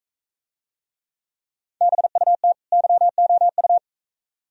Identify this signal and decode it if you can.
Morse code, heard as a keyed tone, about 42 wpm, 700 Hz. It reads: BUT YOU